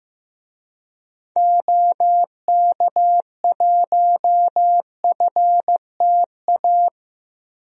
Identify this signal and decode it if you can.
Morse code, heard as a keyed tone, about 16 wpm, 700 Hz. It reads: OK1FTA
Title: DXCC-100 Call Signs - 6 Characters - 15wpm